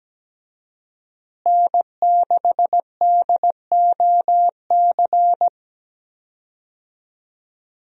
Morse code, heard as a keyed tone, about 17 wpm, 700 Hz. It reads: N6DOC
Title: USA Call Signs - 5 Characters - Repeated 2x - 17wpm